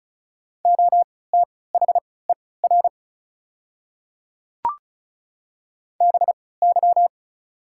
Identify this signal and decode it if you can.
Morse code, heard as a keyed tone, about 36 wpm, spaced as for 20 wpm, 690 Hz. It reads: OTHER BY
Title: 3x Character Spacing - Top 100 Words - 35wpm